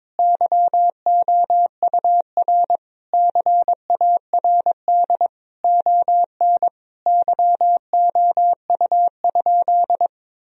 Morse code, heard as a keyed tone, about 23 wpm, 700 Hz. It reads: YOUR CARD ON YOU?
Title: Sentences from Top 1000 Words - Review Part 4 - 22wpm